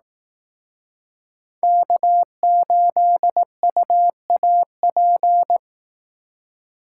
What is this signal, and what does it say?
Morse code, heard as a keyed tone, about 18 wpm, 700 Hz. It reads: K8UAP
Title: Speed Racing - USA 1x3 Call Signs 15wpm